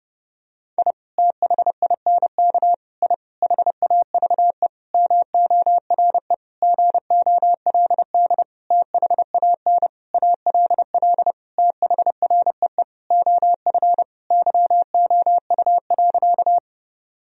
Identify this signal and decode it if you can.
Morse code, heard as a keyed tone, about 32 wpm, 700 Hz. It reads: I THINK I HAVE MORE GOLD THAN ALL THREE OF YOU.
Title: Sentences from Top 1000 Words - Review Part 3 - 30wpm